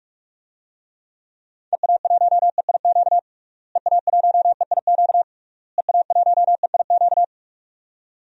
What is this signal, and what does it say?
Morse code, heard as a keyed tone, about 46 wpm, 700 Hz. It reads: EA1EIQ EA1EIQ EA1EIQ
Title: DXCC-100 Call Signs - 6 Characters - Repeated 3x - 45wpm